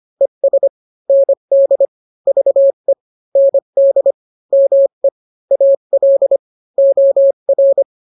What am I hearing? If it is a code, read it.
Morse code, heard as a keyed tone, about 25 wpm, 550 Hz. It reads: ES ND VE ND ME AL OR